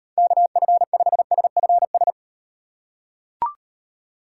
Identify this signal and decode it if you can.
Morse code, heard as a keyed tone, about 38 wpm, 700 Hz. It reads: KF5SFS